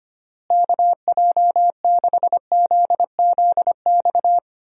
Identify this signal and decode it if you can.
Morse code, heard as a keyed tone, about 25 wpm, 700 Hz. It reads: KJ6ZZX
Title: USA Call Signs - 6 Characters - Repeated 2x - 25wpm